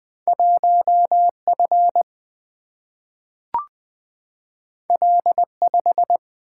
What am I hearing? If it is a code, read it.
Morse code, heard as a keyed tone, about 20 wpm, 700 Hz. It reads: 1F L5
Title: Single-Letter Single-Number 20wpm